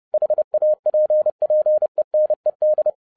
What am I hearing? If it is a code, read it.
Morse code, heard as a keyed tone, about 30 wpm, 610 Hz. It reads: HAPPENED
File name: 2011-07-mp3 / n55 BACK FROM A BIG ACCIDENT 30wpm